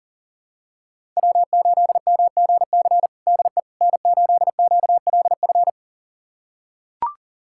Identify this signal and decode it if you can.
Morse code, heard as a keyed tone, about 37 wpm, 700 Hz. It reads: W8MGC DE N8QLF
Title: USA Callsign DE Callsign - 5 Characters - 40wpm